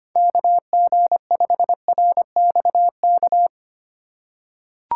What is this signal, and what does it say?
Morse code, heard as a keyed tone, about 25 wpm, 700 Hz. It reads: KG5RXK
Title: USA Call Signs - 6 Characters - 3x Character Spacing - 25wpm